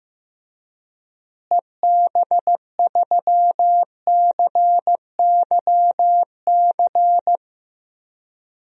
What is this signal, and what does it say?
Morse code, heard as a keyed tone, about 15 wpm, 700 Hz. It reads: EB3CYC
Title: DXCC-100 Call Signs - 6 Characters - Repeated 3x - 15wpm